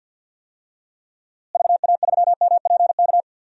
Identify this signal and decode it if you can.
Morse code, heard as a keyed tone, about 53 wpm, 700 Hz. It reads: VA3MPK
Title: DXCC-100 Call Signs - 6 Characters - Repeated 2x - 50wpm